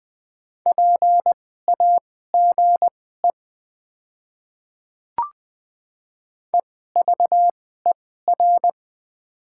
Morse code, heard as a keyed tone, about 20 wpm, spaced as for 14 wpm, 690 Hz. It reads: PAGE EVER
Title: 2x Character Spacing - Top 500 Words - 20wpm